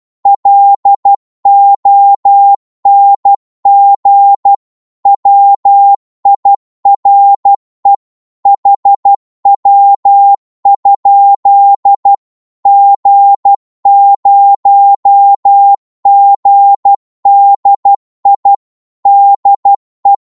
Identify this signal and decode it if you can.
Morse code, heard as a keyed tone, about 12 wpm, 800 Hz. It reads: LONG WIRE HW? G0GDI DE
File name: Morse / qso-12wpm